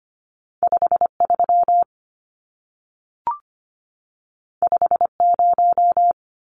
Morse code, heard as a keyed tone, about 25 wpm, 700 Hz. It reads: 53 50
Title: Numbers - 2-Digit 25wpm